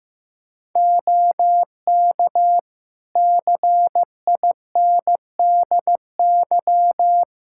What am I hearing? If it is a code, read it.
Morse code, heard as a keyed tone, about 15 wpm, 700 Hz. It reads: OK CINDY
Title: K1USN SST QSO Practice 15wpm